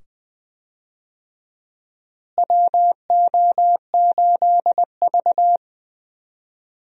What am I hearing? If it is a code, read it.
Morse code, heard as a keyed tone, about 20 wpm, 700 Hz. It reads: WO8V
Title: USA Call Signs - 4 Characters - Repeated 2x - 20wpm